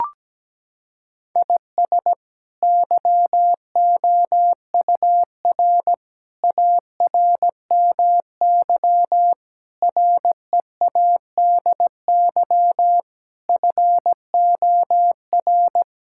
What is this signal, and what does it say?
Morse code, heard as a keyed tone, about 17 wpm, 700 Hz. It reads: IS YOUR ARMY READY FOR
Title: Sentences from Top 1000 Words - Review Part 5 - 17wpm